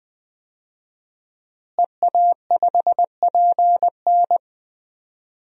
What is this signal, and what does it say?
Morse code, heard as a keyed tone, about 20 wpm, 700 Hz. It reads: EA5PN
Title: DXCC-100 Call Signs - 5 Characters - Repeated 2x - 20wpm